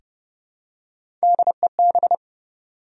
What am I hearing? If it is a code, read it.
Morse code, heard as a keyed tone, about 30 wpm, 700 Hz. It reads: DEB
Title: USA Common Names - Max 3 Letters - 30wpm